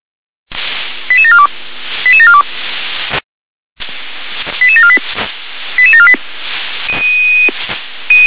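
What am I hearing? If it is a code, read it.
Five-tone selective call (ZVEI), heard as quick runs of five tones, tones of about 70 ms. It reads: 88742 88742 88748 88748
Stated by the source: ZVEI five-tone selective call received by SDR.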